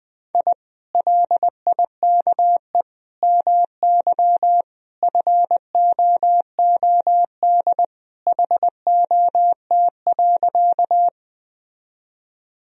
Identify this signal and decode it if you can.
Morse code, heard as a keyed tone, about 20 wpm, 700 Hz. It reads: I LIKE MY FOOD HOT.
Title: Sentences from Top 1000 Words - Review Part 3 - 20wpm